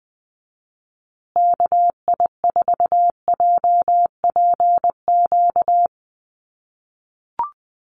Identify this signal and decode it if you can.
Morse code, heard as a keyed tone, about 20 wpm, 700 Hz. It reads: KI4JPQ
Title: USA Call Signs Repeated 3x 20wpm